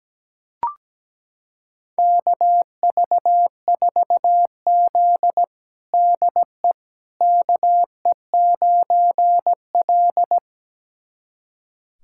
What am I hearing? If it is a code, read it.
Morse code, heard as a keyed tone, about 17 wpm, 700 Hz. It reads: KV4Z DE KE9L
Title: USA Callsign DE Callsign - 4 Characters - 17wpm